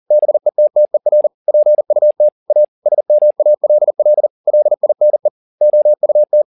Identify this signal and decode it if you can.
Morse code, heard as a keyed tone, about 40 wpm, 600 Hz. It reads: BETTER PUT A SMALL LINE OUT
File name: the old man and the sea 2 40wpm